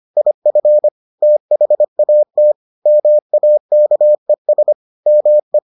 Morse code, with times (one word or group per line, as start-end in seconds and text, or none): 0.2-0.9 IF
1.2-2.5 THAT
2.9-4.7 MAKES
5.1-5.6 ME